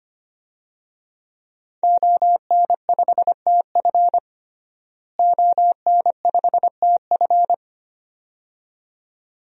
Morse code, heard as a keyed tone, about 25 wpm, 700 Hz. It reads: ON5TF ON5TF
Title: DXCC-100 Call Signs - 5 Characters - Repeated 2x - 25wpm